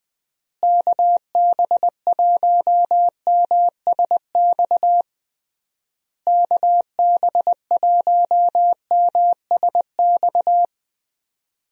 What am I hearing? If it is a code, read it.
Morse code, heard as a keyed tone, about 20 wpm, 700 Hz. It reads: KB1MSX KB1MSX
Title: USA Call Signs - 6 Characters - Repeated 3x - 20wpm